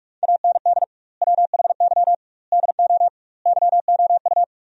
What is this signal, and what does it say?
Morse code, heard as a keyed tone, about 46 wpm, 700 Hz. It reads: AND WHY DO YOU
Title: 2x Word-Spacing - Sentences from Top 400 Words - 45wpm